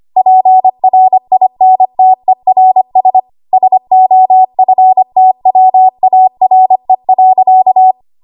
Morse code, heard as a keyed tone, about 25 wpm, 750 Hz. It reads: PRINTERS SOFTWARE.
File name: Archive / 25 WPM / 25WPM